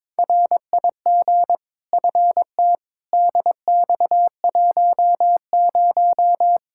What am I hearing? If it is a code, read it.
Morse code, heard as a keyed tone, about 22 wpm, 700 Hz. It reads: RIG FT DX10
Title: Practice QSO - Exchange 2 - Info - 22wpm